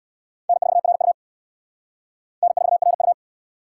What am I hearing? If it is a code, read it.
Morse code, heard as a keyed tone, about 77 wpm, 700 Hz. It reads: N4RU N4RU